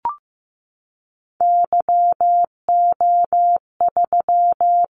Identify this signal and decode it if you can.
Morse code, heard as a keyed tone, about 15 wpm, 700 Hz. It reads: YO3
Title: Callsign Prefixes for Top 25 DXCC - 15wpm